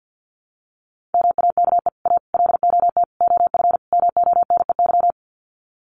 Morse code, heard as a keyed tone, about 50 wpm, 700 Hz. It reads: MAKE A LOT OF MONEY
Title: Sets of 5 Words Encore2 50wpm